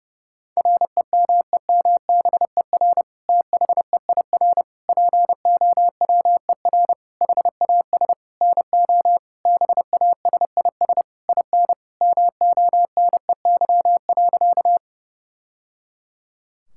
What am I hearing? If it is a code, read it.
Morse code, heard as a keyed tone, about 31 wpm, 700 Hz. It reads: REMEMBER THEIR POWER HAS NO BASIS IN MONEY.
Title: Sentences from Top 1000 Words - Review Part 5 - 30wpm